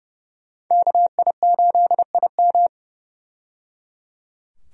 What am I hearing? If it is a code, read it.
Morse code, heard as a keyed tone, about 29 wpm, 700 Hz. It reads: KI8IM
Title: USA Call Signs - 5 Characters - 30wpm